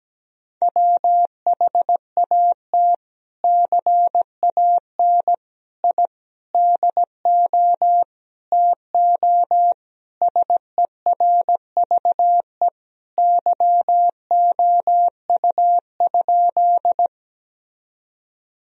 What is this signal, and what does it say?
Morse code, heard as a keyed tone, about 17 wpm, 700 Hz. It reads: WHAT CAN I DO TO SERVE YOU?
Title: Sentences from Top 1000 Words - Review Part 2 - 17wpm